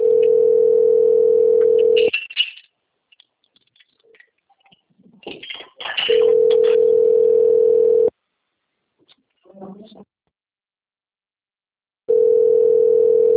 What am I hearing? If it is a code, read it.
Ringback tone (North America), 440 + 480 Hz.